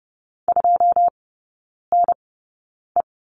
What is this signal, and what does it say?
Morse code, heard as a keyed tone, about 30 wpm, spaced as for 9 wpm, 700 Hz. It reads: 2NE
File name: 081 Lesson 14e - Number 2 - Three Characters 30wpm